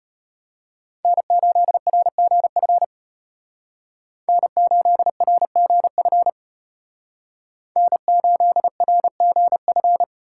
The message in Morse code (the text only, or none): N8RGF N8RGF N8RGF